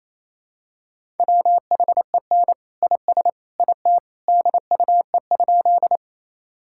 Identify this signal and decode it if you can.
Morse code, heard as a keyed tone, about 28 wpm, 700 Hz. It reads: WHEN IS IT DUE?
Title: Sentences from Top 1000 Words - Review Part 6 - 28wpm